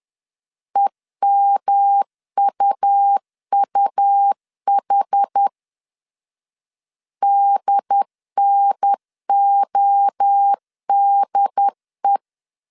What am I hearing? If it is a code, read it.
Morse code, heard as a keyed tone, about 10 wpm, 800 Hz. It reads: EMUUH DNODE
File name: CW / DET / 1 3 1500C S B 10wpm 3335